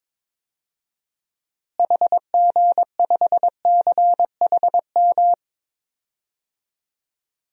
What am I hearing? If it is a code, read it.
Morse code, heard as a keyed tone, about 21 wpm, 700 Hz. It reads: HG5CHM
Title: DXCC-100 Call Signs - 6 Characters - Repeated 3x - 22wpm